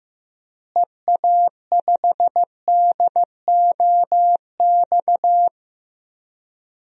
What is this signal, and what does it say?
Morse code, heard as a keyed tone, about 16 wpm, 700 Hz. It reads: EA5DOX
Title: DXCC-100 Call Signs - 6 Characters - Repeated 2x - 15wpm